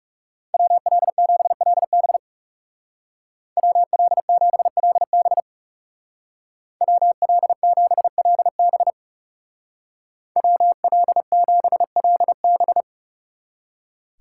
Morse code, heard as a keyed tone, about 35 wpm, 700 Hz. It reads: WL7LB WL7LB WL7LB WL7LB
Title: Speed Racing - USA 2x2 Call Signs 30wpm